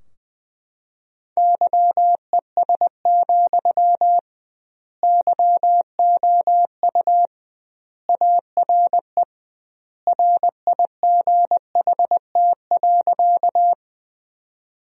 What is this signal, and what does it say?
Morse code, heard as a keyed tone, about 20 wpm, 700 Hz. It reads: YES, YOU ARE RIGHT.
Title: 2x Word-Spacing - Sentences from Top 500 Words - 20wpm